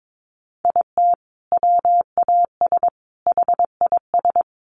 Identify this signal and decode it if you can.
Morse code, heard as a keyed tone, about 22 wpm, 700 Hz. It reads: IT WAS HIS
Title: Sentences from Top 1200 Words 22wpm